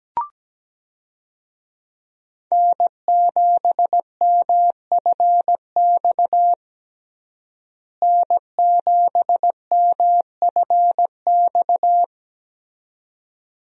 Morse code, heard as a keyed tone, about 17 wpm, 700 Hz. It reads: N7MFX N7MFX
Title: USA Call Signs - 5 Characters - Repeated 2x - 17wpm